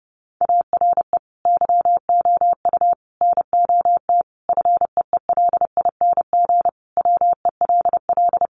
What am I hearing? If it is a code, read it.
Morse code, heard as a keyed tone, about 30 wpm, 700 Hz. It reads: ARE YOU NOT FEELING WELL
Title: Sentences from Top 1200 Words 30wpm